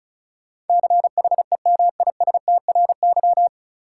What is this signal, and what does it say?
Morse code, heard as a keyed tone, about 36 wpm, 700 Hz. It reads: CHEMISTRY
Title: Sentences from Top 1000 Words - Review Part 6 - New Word Drill - 35wpm